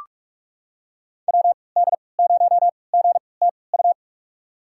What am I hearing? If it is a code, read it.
Morse code, heard as a keyed tone, about 46 wpm, spaced as for 25 wpm, 700 Hz. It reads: WD0GTU
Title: USA Call Signs - 3x Character Spacing - 45wpm